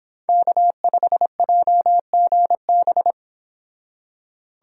Morse code, heard as a keyed tone, about 28 wpm, 700 Hz. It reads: K5JGB